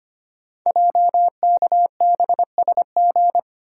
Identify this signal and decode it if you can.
Morse code, heard as a keyed tone, about 25 wpm, 700 Hz. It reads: JKBSG
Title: Code Groups - 5 Letters - 25wpm